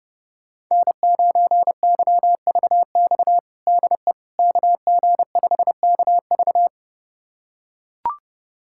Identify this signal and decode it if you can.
Morse code, heard as a keyed tone, about 32 wpm, 700 Hz. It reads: N9YVX DE KG5KV
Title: USA Callsign DE Callsign - 5 Characters - 30wpm